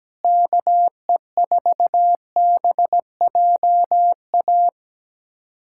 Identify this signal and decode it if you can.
Morse code, heard as a keyed tone, about 17 wpm, 700 Hz. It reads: KE4BJA
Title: USA Callsign DE Callsign - 6 Characters - 17wpm